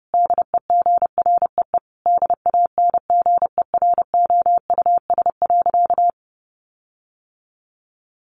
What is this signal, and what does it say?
Morse code, heard as a keyed tone, about 30 wpm, 700 Hz. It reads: DEGREE DANGEROUS.
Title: Sentences from Top 1500 Words 30wpm